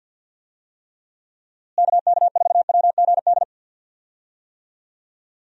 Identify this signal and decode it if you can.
Morse code, heard as a keyed tone, about 51 wpm, 700 Hz. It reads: KK4WGD
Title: USA Call Signs - 6 Characters - Repeated 3x - 50wpm